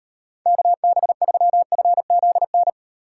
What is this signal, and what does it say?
Morse code, heard as a keyed tone, about 38 wpm, 700 Hz. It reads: KB3FZN